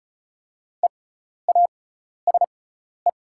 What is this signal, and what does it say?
Morse code, heard as a keyed tone, about 34 wpm, spaced as for 12 wpm, 690 Hz. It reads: EASE
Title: 6x Character Spacing - Top 300 Words - 35wpm